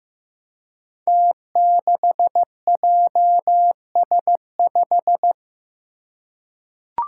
Morse code, heard as a keyed tone, about 15 wpm, 700 Hz. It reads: T6JS5
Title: Five Challenging Letter-Number Combinations 15wpm